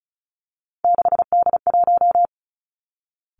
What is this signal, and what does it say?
Morse code, heard as a keyed tone, about 36 wpm, 700 Hz. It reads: 6D1
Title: Three Challenging Letter-Number Combinations 35wpm